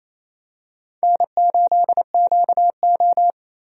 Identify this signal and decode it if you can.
Morse code, heard as a keyed tone, about 28 wpm, 700 Hz. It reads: N8QO
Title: USA Call Signs - 4 Characters - 28wpm